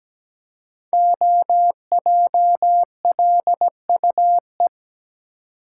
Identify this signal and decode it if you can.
Morse code, heard as a keyed tone, about 17 wpm, 700 Hz. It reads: OJLUE
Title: Code Groups - 5 Letter:Numbers - 17wpm